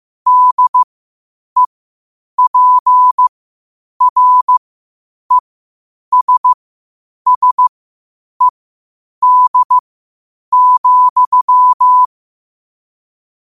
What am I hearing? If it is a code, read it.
Morse code, heard as a keyed tone, about 15 wpm, spaced as for 8 wpm, 1000 Hz. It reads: DEPRESSED,
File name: qotdfiles / 8wpm